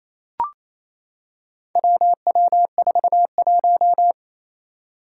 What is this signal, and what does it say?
Morse code, heard as a keyed tone, about 27 wpm, 700 Hz. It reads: WW41
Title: Four Challenging Letter-Number Combinations 28wpm